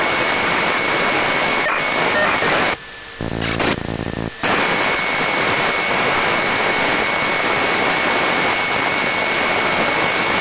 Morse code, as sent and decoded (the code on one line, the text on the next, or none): none